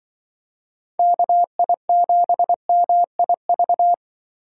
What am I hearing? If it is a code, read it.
Morse code, heard as a keyed tone, about 24 wpm, 700 Hz. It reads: KI7MIV
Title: Speed Racing - USA 2x3 Call Signs 20wpm